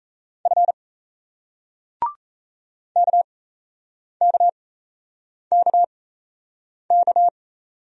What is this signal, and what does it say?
Morse code, heard as a keyed tone, about 38 wpm, spaced as for 8 wpm, 700 Hz. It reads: F KKKK